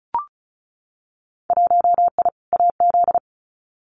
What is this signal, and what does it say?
Morse code, heard as a keyed tone, about 36 wpm, 700 Hz. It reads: 1I AZ
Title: Winter Field Day Exchanges 35wpm